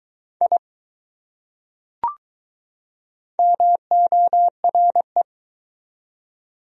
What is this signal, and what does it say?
Morse code, heard as a keyed tone, about 20 wpm, 700 Hz. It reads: I MORE